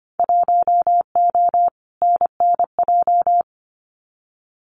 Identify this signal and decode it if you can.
Morse code, heard as a keyed tone, about 25 wpm, 700 Hz. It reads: 1O NNJ
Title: Winter Field Day Exchanges 25wpm